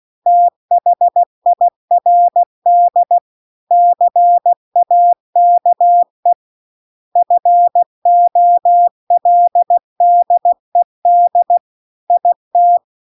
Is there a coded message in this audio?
Yes: Morse code from 0.3 to 12.8 s, heard as a keyed tone, about 16 wpm, 700 Hz. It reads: THIRD CAKE FOLDED IT